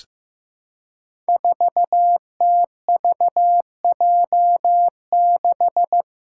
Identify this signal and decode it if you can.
Morse code, heard as a keyed tone, about 15 wpm, 700 Hz. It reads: 4TVJ6
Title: Five Challenging Letter-Number Combinations 15wpm